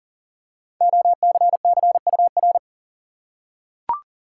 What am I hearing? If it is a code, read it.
Morse code, heard as a keyed tone, about 40 wpm, 700 Hz. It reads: OCCUR
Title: Five Letter Words - 40wpm